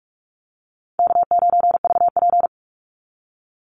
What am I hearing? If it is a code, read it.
Morse code, heard as a keyed tone, about 44 wpm, 700 Hz. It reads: K9VP
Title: Call Signs Repeated 2x - 45wpm